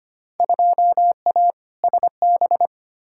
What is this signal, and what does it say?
Morse code, heard as a keyed tone, about 25 wpm, 700 Hz. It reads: 2A SB
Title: Field Day Exchanges - 25wpm